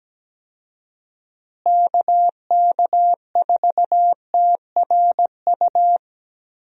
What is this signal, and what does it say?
Morse code, heard as a keyed tone, about 17 wpm, 700 Hz. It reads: KK4TRU
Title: USA Call Signs - 6 Characters - Repeated 3x - 17wpm